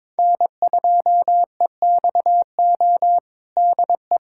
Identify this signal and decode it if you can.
Morse code, heard as a keyed tone, about 22 wpm, 700 Hz. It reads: N2EXO DE
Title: USA Callsign DE Callsign - 5 Characters - 22wpm